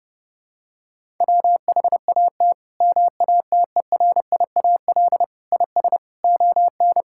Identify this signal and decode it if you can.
Morse code, heard as a keyed tone, about 32 wpm, 700 Hz. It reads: WHAT MATERIAL IS ON
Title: Sentences from Top 1000 Words - Review Part 5 - 30wpm